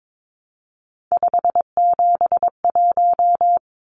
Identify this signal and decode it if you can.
Morse code, heard as a keyed tone, about 22 wpm, 700 Hz. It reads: 571
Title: Numbers - 3-Digit 22wpm